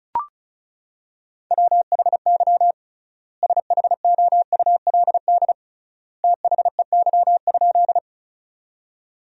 Morse code, heard as a keyed tone, about 36 wpm, 700 Hz. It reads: WHY SHOULD THEY?
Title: 3x Word-Spacing - Sentences from Top 300 Words - 35wpm